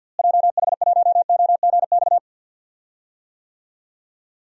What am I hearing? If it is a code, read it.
Morse code, heard as a keyed tone, about 47 wpm, 700 Hz. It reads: JH1OGX
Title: DXCC-100 Call Signs - 6 Characters - Repeated 3x - 50wpm